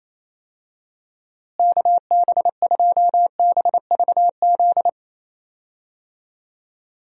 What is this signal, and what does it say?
Morse code, heard as a keyed tone, about 28 wpm, 700 Hz. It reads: KB2BVZ
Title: USA Call Signs - 6 Characters - Repeated 2x - 28wpm